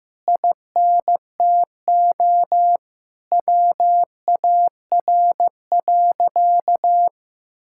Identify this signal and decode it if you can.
Morse code, heard as a keyed tone, about 15 wpm, 700 Hz. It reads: INTO WAR.
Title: Sentences from Top 1000 Words - Review Part 5 - 15wpm